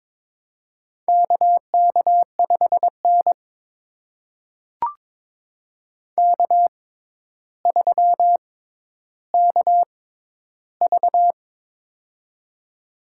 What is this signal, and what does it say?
Morse code, heard as a keyed tone, about 22 wpm, spaced as for 11 wpm, 700 Hz. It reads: KK5N K3KV
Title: USA Call Signs - 4 Characters - 6x Character Spacing - 22wpm